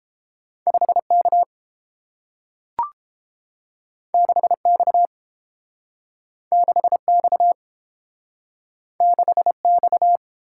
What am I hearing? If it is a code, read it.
Morse code, heard as a keyed tone, about 30 wpm, 700 Hz. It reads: 5 K 6 X 6 X 6 X